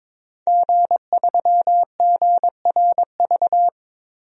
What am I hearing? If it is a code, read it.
Morse code, heard as a keyed tone, about 22 wpm, 700 Hz. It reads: G3GRV
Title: DXCC-100 Call Signs - 5 Characters - Repeated 3x - 22wpm